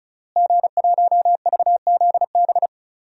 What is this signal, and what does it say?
Morse code, heard as a keyed tone, about 36 wpm, 700 Hz. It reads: G1VZB
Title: DXCC-100 Call Signs - 5 Characters - Repeated 3x - 35wpm